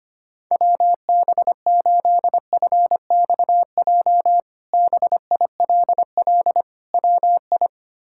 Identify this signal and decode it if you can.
Morse code, heard as a keyed tone, about 25 wpm, 700 Hz. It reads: WB8FXJ BILL WI
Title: K1USN SST QSO Practice - 25wpm